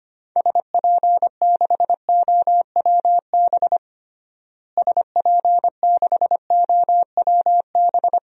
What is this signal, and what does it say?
Morse code, heard as a keyed tone, about 25 wpm, 700 Hz. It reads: SP6OWB SP6OWB
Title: DXCC-100 Call Signs - 6 Characters - Repeated 2x - 25wpm